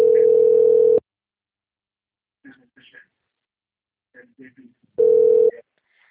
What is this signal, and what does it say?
Ringback tone (North America), 440 + 480 Hz, with speech.
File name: rg-1600-unknown- - - .55417